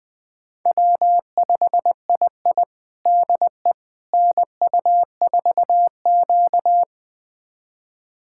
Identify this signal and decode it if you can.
Morse code, heard as a keyed tone, about 21 wpm, 700 Hz. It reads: W5II DE NU4Q
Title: USA Callsign DE Callsign - 4 Characters - 20wpm